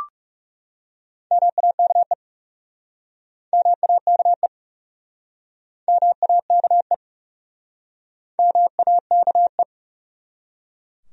Morse code, heard as a keyed tone, about 41 wpm, 700 Hz. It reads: MAKE MAKE MAKE MAKE